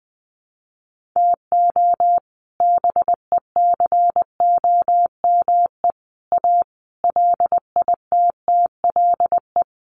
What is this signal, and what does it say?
Morse code, heard as a keyed tone, about 20 wpm, 700 Hz. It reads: TO BECOME A LITTLE
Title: Sentences from Top 1300 Words 20wpm